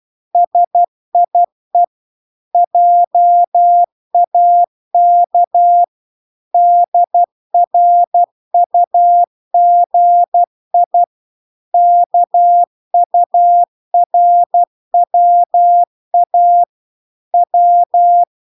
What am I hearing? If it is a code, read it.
Morse code, heard as a keyed tone, about 12 wpm, 700 Hz. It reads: SIE JAK DRUGI KURWA W